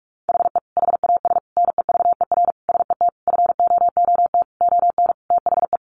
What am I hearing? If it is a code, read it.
Morse code, heard as a keyed tone, about 44 wpm, 700 Hz. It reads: HE HAS NEVER SET FOOT ON THE
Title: Sentences from Top 1500 Words 45wpm